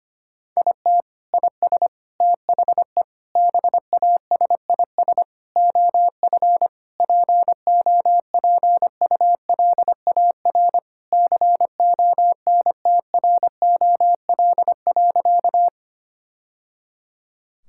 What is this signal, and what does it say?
Morse code, heard as a keyed tone, about 25 wpm, 700 Hz. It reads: IT IS THE BASIS OF POPULAR CONTROL.
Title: Sentences from Top 1000 Words - Review Part 5 - 25wpm